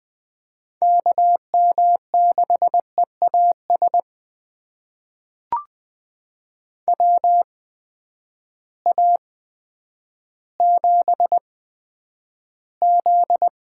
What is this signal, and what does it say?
Morse code, heard as a keyed tone, about 20 wpm, spaced as for 10 wpm, 700 Hz. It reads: KM6EAS WA7Z
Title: USA Call Signs - 6 Characters - 8x Character Spacing - 20wpm